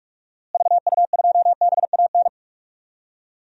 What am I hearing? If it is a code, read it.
Morse code, heard as a keyed tone, about 46 wpm, 700 Hz. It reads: VU2BAN
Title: DXCC-100 Call Signs - 6 Characters - Repeated 3x - 45wpm